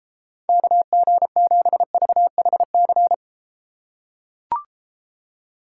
Morse code, heard as a keyed tone, about 32 wpm, 700 Hz. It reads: KG7VHC